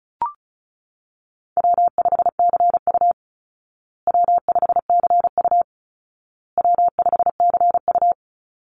Morse code, heard as a keyed tone, about 34 wpm, 700 Hz. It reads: W5CU W5CU W5CU
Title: Call Signs Repeated 2x - 35wpm